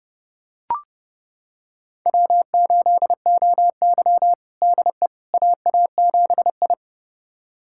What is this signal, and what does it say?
Morse code, heard as a keyed tone, about 30 wpm, 700 Hz. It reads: W8OY DE AA7I
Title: USA Callsign DE Callsign - 4 Characters - 30wpm